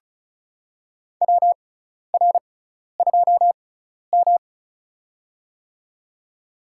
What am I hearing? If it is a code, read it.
Morse code, heard as a keyed tone, about 34 wpm, spaced as for 12 wpm, 700 Hz. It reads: WR2M
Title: USA Call Signs - 4 Characters - 6x Character Spacing - 35wpm